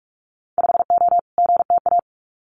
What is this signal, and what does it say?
Morse code, heard as a keyed tone, about 46 wpm, 700 Hz. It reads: 5O GTA
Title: Winter Field Day Exchanges Encore 45wpm